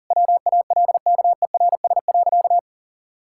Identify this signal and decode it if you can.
Morse code, heard as a keyed tone, about 38 wpm, 700 Hz. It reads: WALKERS.
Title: Pangram Challenge v2 - 40wpm